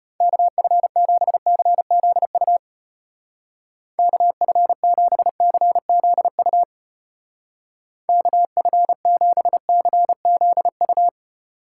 Morse code, heard as a keyed tone, about 33 wpm, 700 Hz. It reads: KF7CZU KF7CZU KF7CZU